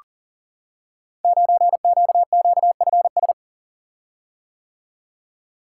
Morse code, heard as a keyed tone, about 40 wpm, 700 Hz. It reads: 9QQFS
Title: Code Groups - 5 Letter:Numbers - 40wpm